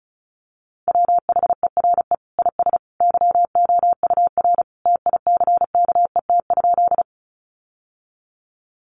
Morse code, heard as a keyed tone, about 34 wpm, 700 Hz. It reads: WHERE IS YOUR TICKET?
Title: Sentences from Top 1400 Words 35wpm